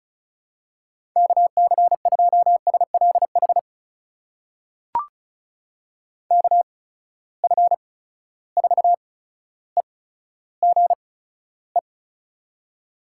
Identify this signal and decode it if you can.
Morse code, heard as a keyed tone, about 36 wpm, spaced as for 15 wpm, 700 Hz. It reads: KC2SLH KF4EGE
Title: USA Call Signs - 6 Characters - 8x Character Spacing - 35wpm